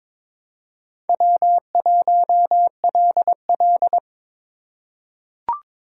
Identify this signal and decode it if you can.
Morse code, heard as a keyed tone, about 22 wpm, 700 Hz. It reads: W1LL
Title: USA Call Signs - 4 Characters - 3x Character Spacing - 22wpm